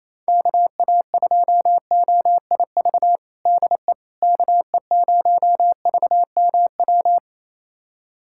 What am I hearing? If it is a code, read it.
Morse code, heard as a keyed tone, about 28 wpm, 700 Hz. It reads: KA2OIV DE KE0VMW
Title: USA Callsign DE Callsign - 6 Characters - 28wpm